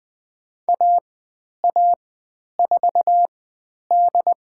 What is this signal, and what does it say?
Morse code, heard as a keyed tone, about 20 wpm, spaced as for 10 wpm, 700 Hz. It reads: AA4D
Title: 102 Lesson 17g - Letter u - Call Signs 20wpm